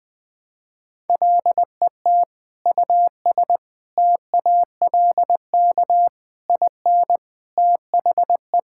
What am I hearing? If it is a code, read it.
Morse code, heard as a keyed tone, about 20 wpm, 700 Hz. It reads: LET US TALK IN THE
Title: Sentences from Top 1600 Words 20wpm